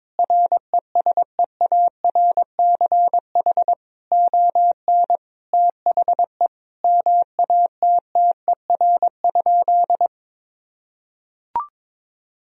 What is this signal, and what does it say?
Morse code, heard as a keyed tone, about 21 wpm, 700 Hz. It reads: RESEARCH ON THE MATTER?
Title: Sentences from Top 1000 Words - Review Part 4 - 22wpm